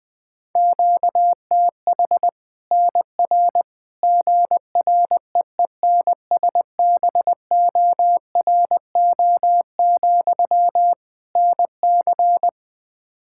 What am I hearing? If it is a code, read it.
Morse code, heard as a keyed tone, about 21 wpm, 700 Hz. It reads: QTH NR GREENSBORO, NC
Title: Practice QSO - Exchange 1 - Beginning - 20wpm